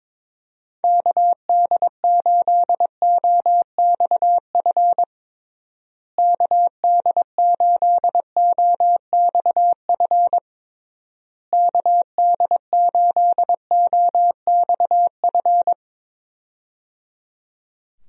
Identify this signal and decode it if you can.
Morse code, heard as a keyed tone, about 22 wpm, 700 Hz. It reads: KD8OXF KD8OXF KD8OXF
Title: USA Call Signs - 6 Characters - Repeated 3x - 22wpm